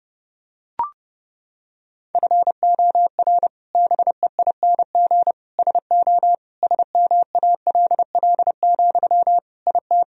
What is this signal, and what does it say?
Morse code, heard as a keyed tone, about 30 wpm, 700 Hz. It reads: FOR BEING SO SMALL, IT
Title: Sentences from Top 1000 Words - Review Part 3 - 30wpm